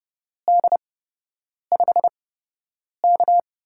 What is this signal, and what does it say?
Morse code, heard as a keyed tone, about 30 wpm, spaced as for 8 wpm, 700 Hz. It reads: D5K
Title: USA Call Signs - 8x Character Spacing - 30wpm